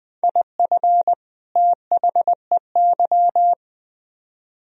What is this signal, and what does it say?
Morse code, heard as a keyed tone, about 25 wpm, 700 Hz. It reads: IF THEY